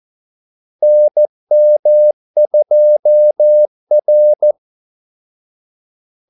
Morse code, heard as a keyed tone, about 14 wpm, 600 Hz. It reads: NM2R
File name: Callsigns-14wpm